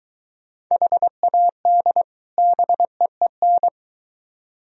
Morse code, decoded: HAD BEEN

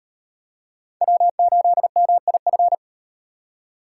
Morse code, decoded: W8MIF

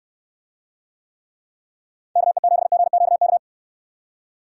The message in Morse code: KE7GQK

